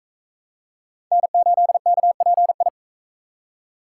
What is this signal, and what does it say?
Morse code, heard as a keyed tone, about 44 wpm, 700 Hz. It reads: N8KPI